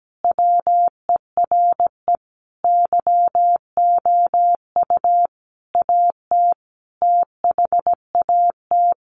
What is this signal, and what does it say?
Morse code, heard as a keyed tone, about 17 wpm, 700 Hz. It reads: WERE YOU AT THAT
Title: Sentences from Top 200 Words 17wpm